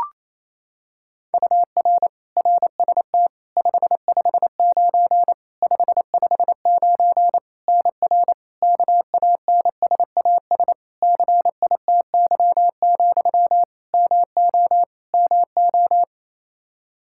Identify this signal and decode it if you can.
Morse code, heard as a keyed tone, about 28 wpm, 700 Hz. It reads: UR RST 559 559 NR KANSAS CITY, MO MO
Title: K1USN SST QSO Practice 28wpm